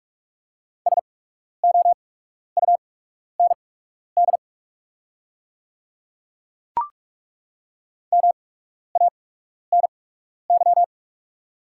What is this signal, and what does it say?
Morse code, heard as a keyed tone, about 43 wpm, spaced as for 12 wpm, 700 Hz. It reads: SOUND MANY
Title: 8x Character Spacing - Top 100 Words - 45wpm